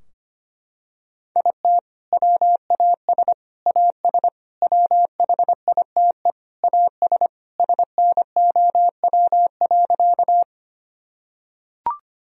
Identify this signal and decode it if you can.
Morse code, heard as a keyed tone, about 25 wpm, 700 Hz. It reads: IT WAS AS WHITE AS SNOW.
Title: Sentences from Top 1000 Words - Review Part 3 - 25wpm